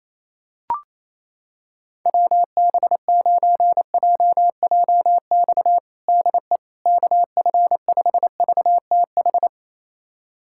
Morse code, heard as a keyed tone, about 28 wpm, 700 Hz. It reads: WB9JJX DE KF5VTH
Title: USA Callsign DE Callsign - 6 Characters - 28wpm